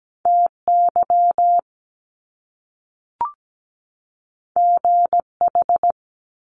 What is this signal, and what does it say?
Morse code, heard as a keyed tone, about 17 wpm, 700 Hz. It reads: TY GH
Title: Top Two Letter Combinations in Words 17wpm